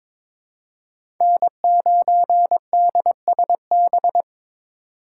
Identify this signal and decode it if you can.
Morse code, heard as a keyed tone, about 22 wpm, 700 Hz. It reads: N9DSB
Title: USA Call Signs - 4x Character Spacing - 22wpm